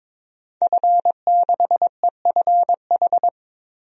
Morse code, heard as a keyed tone, about 22 wpm, 700 Hz. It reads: F6EFH
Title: DXCC-100 Call Signs - 5 Characters - 22wpm